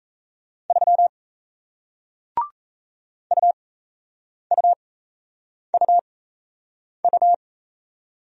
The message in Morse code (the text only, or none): 3 UUUU